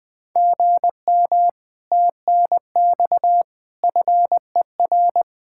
Morse code, heard as a keyed tone, about 20 wpm, 700 Hz. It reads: GM TNX FER
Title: Speed-Racing Common QSO Phrases Encore - 20wpm